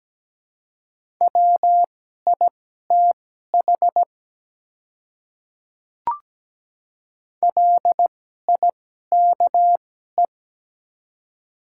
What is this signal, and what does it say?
Morse code, heard as a keyed tone, about 17 wpm, spaced as for 12 wpm, 700 Hz. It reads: WITH LIKE
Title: 2x Character Spacing - Top 100 Words - 17wpm